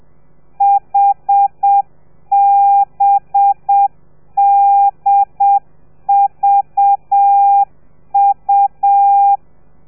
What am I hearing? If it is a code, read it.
Morse code, heard as a keyed tone, about 7 wpm, 800 Hz. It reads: HBDVU
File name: telegrafia / i2tfj / Lez4 30